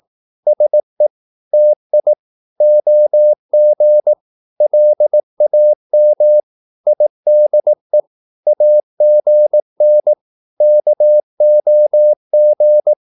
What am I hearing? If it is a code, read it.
Morse code, heard as a keyed tone, about 18 wpm, 600 Hz. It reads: SE TI OG LAM IDE AGN KOG
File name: Korte ord DK 18wpm